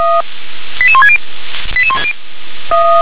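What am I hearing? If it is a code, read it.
Five-tone selective call (ZVEI), heard as quick runs of five tones, tones of about 70 ms. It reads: 88160
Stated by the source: ZVEI five-tone call received by SDR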